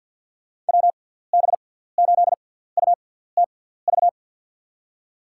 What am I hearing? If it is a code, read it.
Morse code, heard as a keyed tone, about 51 wpm, spaced as for 17 wpm, 700 Hz. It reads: WB8UTV
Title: USA Call Signs - 6 Characters - 6x Character Spacing - 50wpm